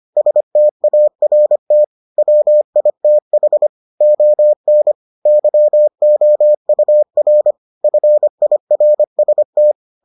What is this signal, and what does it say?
Morse code, heard as a keyed tone, about 25 wpm, 600 Hz. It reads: START WITH ON YOUR FIRST